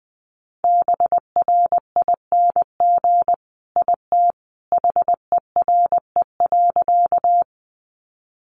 Morse code, heard as a keyed tone, about 20 wpm, 700 Hz. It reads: BRING IT HERE.
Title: Sentences from Top 500 Words 20wpm